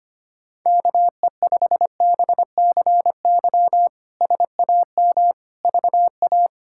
Morse code, heard as a keyed tone, about 25 wpm, 700 Hz. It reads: KE5BCY SAM VA
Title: K1USN SST QSO Practice - 25wpm